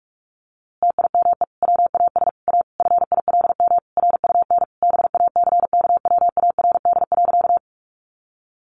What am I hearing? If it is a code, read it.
Morse code, heard as a keyed tone, about 44 wpm, 700 Hz. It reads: TIME WAS A FILM RUN BACKWARD.
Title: A Sound Of Thunder 45wpm